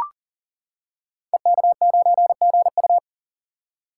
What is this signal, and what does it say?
Morse code, heard as a keyed tone, about 38 wpm, 700 Hz. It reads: EK9GU
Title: Code Groups - 5 Letter:Numbers - 40wpm